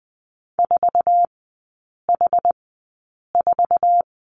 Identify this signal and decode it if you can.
Morse code, heard as a keyed tone, about 20 wpm, spaced as for 8 wpm, 690 Hz. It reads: 4H4
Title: 081 Lesson 14e - Number 2 - Three Characters 20wpm